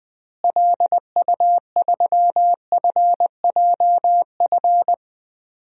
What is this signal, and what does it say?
Morse code, heard as a keyed tone, about 21 wpm, 700 Hz. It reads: LU3FJF
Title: DXCC-100 Call Signs - 6 Characters - Repeated 2x - 20wpm